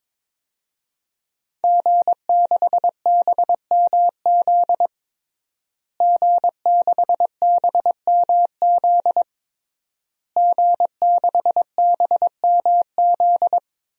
Morse code, heard as a keyed tone, about 22 wpm, 700 Hz. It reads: G6BMZ G6BMZ G6BMZ
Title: DXCC-100 Call Signs - 5 Characters - Repeated 3x - 22wpm